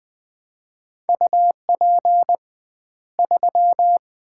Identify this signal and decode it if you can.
Morse code, heard as a keyed tone, about 20 wpm, 700 Hz. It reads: UP 3
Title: Common QSO Phrases - 2x Word-Spacing - 20wpm